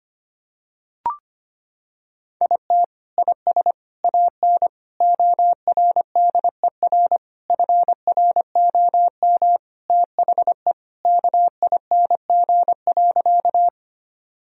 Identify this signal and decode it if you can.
Morse code, heard as a keyed tone, about 25 wpm, 700 Hz. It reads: IT IS AN ORDER FROM THE KING.
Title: Sentences from Top 1000 Words - Review Part 2 - 25wpm